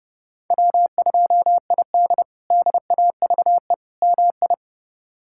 Morse code, heard as a keyed tone, about 30 wpm, 700 Hz. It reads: W2ID DAVE MI
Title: K1USN SST QSO Practice - 30wpm